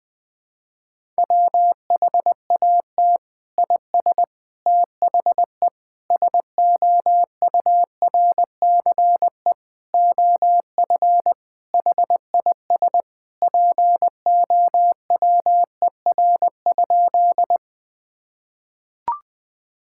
Morse code, heard as a keyed tone, about 20 wpm, 700 Hz. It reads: WHAT IS THE SOURCE OF HIS POWER?
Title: Sentences from Top 1000 Words - Review Part 5 - 20wpm